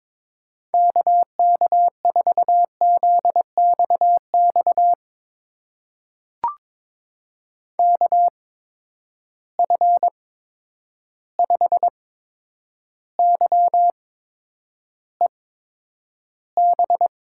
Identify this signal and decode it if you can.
Morse code, heard as a keyed tone, about 22 wpm, spaced as for 9 wpm, 700 Hz. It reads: KK4ZXX KF5YEB
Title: USA Call Signs - 6 Characters - 8x Character Spacing - 22wpm